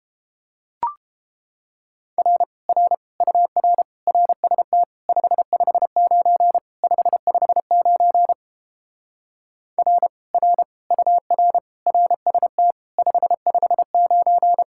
Morse code, heard as a keyed tone, about 29 wpm, 700 Hz. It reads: R R UR RST 559 559 R R UR RST 559